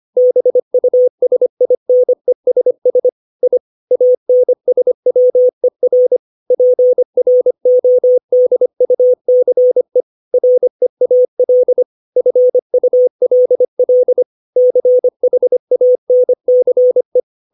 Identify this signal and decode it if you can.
Morse code, heard as a keyed tone, about 25 wpm, 500 Hz. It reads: BUSINESS I ANSWER PRODUCE REAL FULL CHANCE
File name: 632 words / 632 Words 25WPM